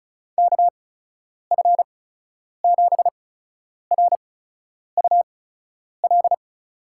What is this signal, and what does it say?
Morse code, heard as a keyed tone, about 36 wpm, spaced as for 10 wpm, 700 Hz. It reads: KF7RUL
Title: USA Call Signs - 6 Characters - 8x Character Spacing - 35wpm